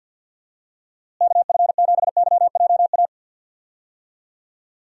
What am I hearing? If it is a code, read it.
Morse code, heard as a keyed tone, about 49 wpm, 700 Hz. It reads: KF7YJA
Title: USA Call Signs - 6 Characters - Repeated 3x - 50wpm